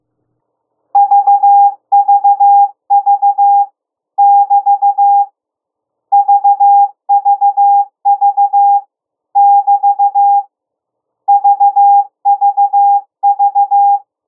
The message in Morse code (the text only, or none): VVV <BT> VVV <BT> VVV